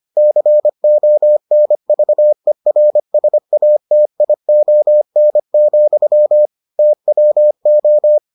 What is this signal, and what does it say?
Morse code, heard as a keyed tone, about 25 wpm, 600 Hz. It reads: CONVERSATION, TWO